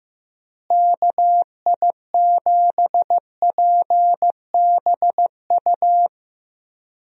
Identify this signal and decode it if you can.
Morse code, heard as a keyed tone, about 15 wpm, 700 Hz. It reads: KI7PBU
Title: USA Call Signs - 6 Characters - Repeated 3x - 15wpm